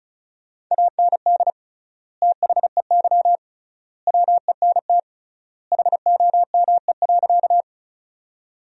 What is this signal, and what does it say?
Morse code, heard as a keyed tone, about 36 wpm, 700 Hz. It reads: AND THEY WENT HOME.
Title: 3x Word-Spacing - Sentences from Top 200 Words - 35wpm